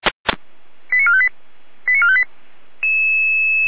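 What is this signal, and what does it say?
Five-tone selective call (ZVEI), heard as quick runs of five tones, tones of about 70 ms. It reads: 89458 89458
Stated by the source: ZVEI five-tone selective call received on an SDR.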